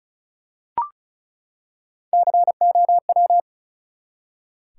Morse code, heard as a keyed tone, about 36 wpm, 700 Hz. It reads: COW
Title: Three Letter Words - 35wpm